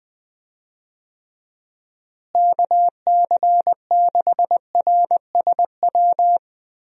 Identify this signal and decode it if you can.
Morse code, heard as a keyed tone, about 20 wpm, 700 Hz. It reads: KC6RSW
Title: USA Call Signs - 6 Characters - Repeated 3x - 20wpm